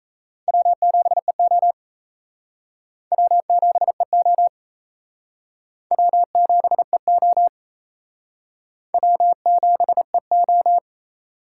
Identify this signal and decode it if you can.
Morse code, heard as a keyed tone, about 37 wpm, 700 Hz. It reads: W7EO W7EO W7EO W7EO